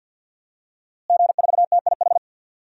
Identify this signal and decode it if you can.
Morse code, heard as a keyed tone, about 53 wpm, 700 Hz. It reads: G4TIH
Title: DXCC-100 Call Signs - 5 Characters - Repeated 2x - 50wpm